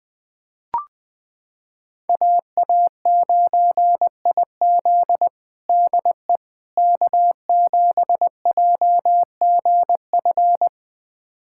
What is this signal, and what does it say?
Morse code, heard as a keyed tone, about 20 wpm, 700 Hz. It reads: AA9IZ DE K7JGF
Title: USA Callsign DE Callsign - 5 Characters - 20wpm